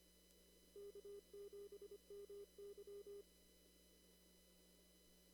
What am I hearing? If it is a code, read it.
Morse code, heard as a keyed tone, about 25 wpm, 400 Hz. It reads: K7MY
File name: CW / kt5j wpx cw-op-k5pi